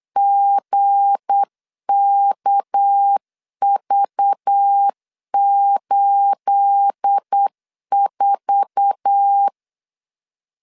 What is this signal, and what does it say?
Morse code, heard as a keyed tone, about 8 wpm, 800 Hz. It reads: GKV84
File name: CW / DET / 1 9 1500C S A 8wpm 5637